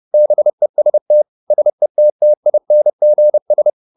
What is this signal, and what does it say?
Morse code, heard as a keyed tone, about 30 wpm, 600 Hz. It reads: BEST SETTINGS